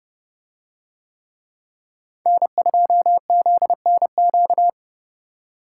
Morse code, heard as a keyed tone, about 29 wpm, 700 Hz. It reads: N2ZNQ